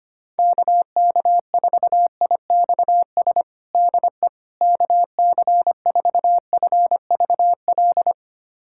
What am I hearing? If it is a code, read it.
Morse code, heard as a keyed tone, about 25 wpm, 700 Hz. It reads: KK4IXS DE KC4FVL
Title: USA Callsign DE Callsign - 6 Characters - 25wpm